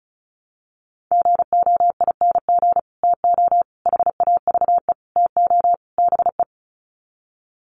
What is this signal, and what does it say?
Morse code, heard as a keyed tone, about 34 wpm, 700 Hz. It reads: GOING TO HAVE TO BE
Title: Sets of 5 Words 35wpm